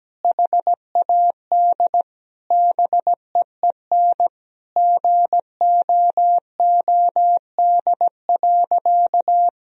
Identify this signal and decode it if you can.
Morse code, heard as a keyed tone, about 17 wpm, 700 Hz. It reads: HAD BEEN GOOD.
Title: Sentences from Top 1000 Words - Review Part 1 - 17wpm